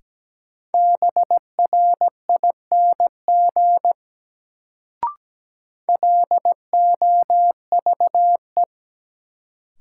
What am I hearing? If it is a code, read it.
Morse code, heard as a keyed tone, about 17 wpm, 700 Hz. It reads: BRING LOVE
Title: Sets of 5 Words Encore - New Word Drill - 17wpm